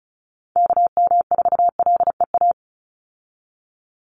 Morse code, heard as a keyed tone, about 34 wpm, 700 Hz. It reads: KM4LEA
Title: USA Call Signs 35wpm